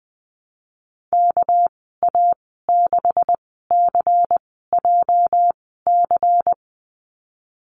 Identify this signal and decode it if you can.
Morse code, heard as a keyed tone, about 20 wpm, spaced as for 15 wpm, 700 Hz. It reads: KA6CJC
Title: 231 Lesson 37g - Letter x - Call Signs 20wpm